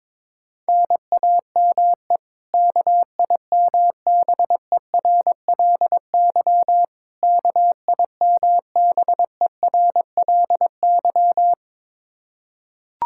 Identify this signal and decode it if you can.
Morse code, heard as a keyed tone, about 22 wpm, 700 Hz. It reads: NAME KIMBERLY KIMBERLY
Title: Practice QSO - Exchange 1 - Beginning - 22wpm